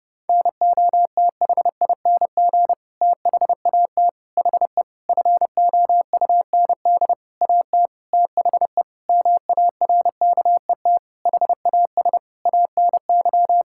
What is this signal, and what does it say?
Morse code, heard as a keyed tone, about 30 wpm, 700 Hz. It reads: NOTHING THAT HE FOUND AT THE MARKET HAS ANY
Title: Sentences from Top 1000 Words - Review Part 6 - 30wpm